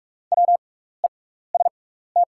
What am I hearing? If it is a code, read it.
Morse code, heard as a keyed tone, about 44 wpm, spaced as for 15 wpm, 700 Hz. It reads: WEST
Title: 6x Character Spacing - Top 500 Words - 45wpm